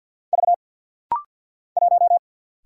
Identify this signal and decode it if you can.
Morse code, heard as a keyed tone, about 50 wpm, 690 Hz. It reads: V 1
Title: Single Letter-Number Alt - Rapid-Fire 50wpm